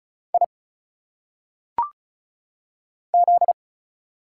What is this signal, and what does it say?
Morse code, heard as a keyed tone, about 34 wpm, 700 Hz. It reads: I Z
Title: Single Letters - Phonetic Pronunciation 35wpm